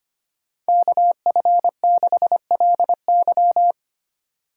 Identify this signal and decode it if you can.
Morse code, heard as a keyed tone, about 25 wpm, 700 Hz. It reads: KF6LY
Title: USA Call Signs - 5 Characters - 4x Character Spacing - 25wpm